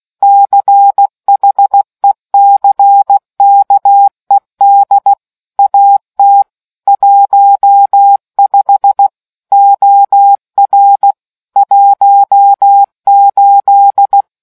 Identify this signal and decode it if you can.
Morse code, heard as a keyed tone, about 16 wpm, 800 Hz. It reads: CHECKED AT 15 OR 18